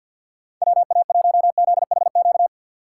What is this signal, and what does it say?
Morse code, heard as a keyed tone, about 52 wpm, 700 Hz. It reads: WA1ZHQ
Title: USA Call Signs - 6 Characters - 6x Character Spacing - 50wpm